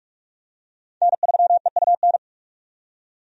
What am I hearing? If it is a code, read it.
Morse code, heard as a keyed tone, about 48 wpm, 700 Hz. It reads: N3EUN
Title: USA Call Signs - 5 Characters - 45wpm